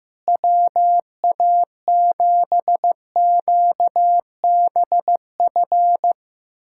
Morse code, heard as a keyed tone, about 15 wpm, 700 Hz. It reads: WA7QBF
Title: USA Callsign DE Callsign 15wpm